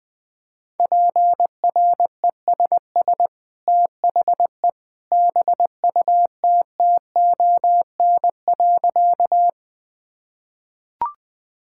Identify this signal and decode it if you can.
Morse code, heard as a keyed tone, about 21 wpm, 700 Hz. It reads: PRESS THE BUTTON.
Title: Easy Sentences with Hints - 20wpm